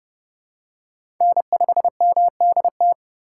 Morse code, heard as a keyed tone, about 31 wpm, 700 Hz. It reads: N5MDT
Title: USA Call Signs - 2x Character Spacing - 30wpm